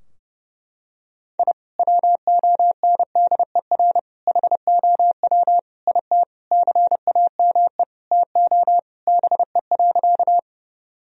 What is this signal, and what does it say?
Morse code, heard as a keyed tone, about 30 wpm, 700 Hz. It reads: I WONDER HOW IT CAME TO BE.
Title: Sentences from Top 1000 Words - Review Part 3 - 30wpm